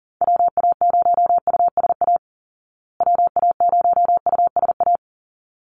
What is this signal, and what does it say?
Morse code, heard as a keyed tone, about 40 wpm, 700 Hz. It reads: WA0USA WA0USA
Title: Call Signs Repeated 3x - 40wpm